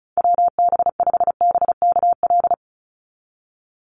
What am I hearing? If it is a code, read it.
Morse code, heard as a keyed tone, about 36 wpm, 700 Hz. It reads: WB5BKL
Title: NA SOTA Chaser Callsigns 35wpm